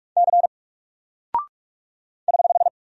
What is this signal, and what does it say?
Morse code, heard as a keyed tone, about 44 wpm, 700 Hz. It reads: C <HH>
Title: Single Character (Letter:Number:Punctuation:Prosign) 45wpm